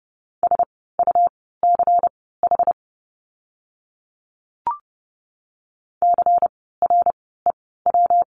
Morse code, heard as a keyed tone, about 30 wpm, spaced as for 17 wpm, 700 Hz. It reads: SUCH CREW
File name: course / 30:10wpm / 115 Lesson 19f - Letter w - Words 30wpm